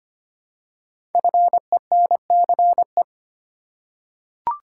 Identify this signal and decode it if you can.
Morse code, heard as a keyed tone, about 25 wpm, 700 Hz. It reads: FENCE
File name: 5 Letter Words - First 3 Letters Given - 25wpm